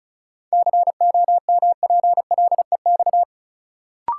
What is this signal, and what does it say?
Morse code, heard as a keyed tone, about 34 wpm, 700 Hz. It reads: COMPLEX
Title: Sentences from Top 900 Words - New Word Drill - 35wpm